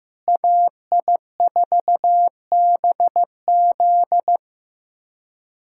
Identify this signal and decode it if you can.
Morse code, heard as a keyed tone, about 15 wpm, 700 Hz. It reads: AI4BZ
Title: USA Callsign DE Callsign 15wpm